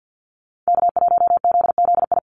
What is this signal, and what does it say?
Morse code, heard as a keyed tone, about 50 wpm, 700 Hz. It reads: K1ZZI
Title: Call Signs - 50wpm